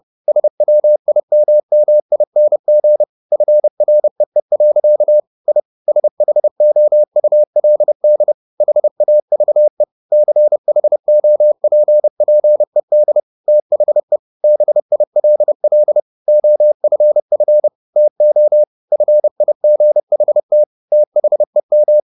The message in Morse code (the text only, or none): SWIMMING FREE. I SHOULD HAVE CHOPPED THE BILL OFF TO FIGHT THEM